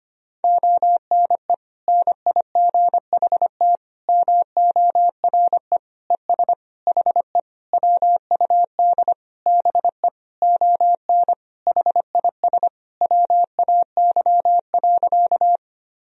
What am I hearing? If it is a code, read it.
Morse code, heard as a keyed tone, about 25 wpm, 700 Hz. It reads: ONE NIGHT MORE ES HE WUD BE ON HIS WAY.
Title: Sentences from Top 300 Words and Abbreviations 25wpm